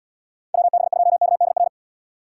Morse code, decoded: WB2KDR

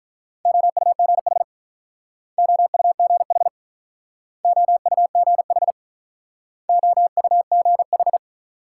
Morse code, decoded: OUGH OUGH OUGH OUGH

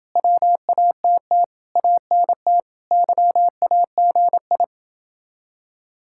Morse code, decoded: WATT ANT YAGI

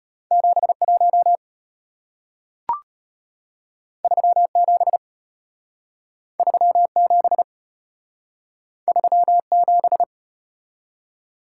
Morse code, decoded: 71 37 37 37

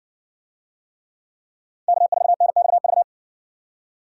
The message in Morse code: K4NXV

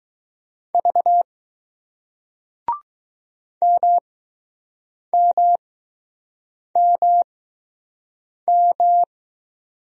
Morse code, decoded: V MMMM

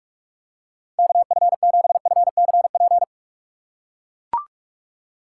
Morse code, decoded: KF7FCP

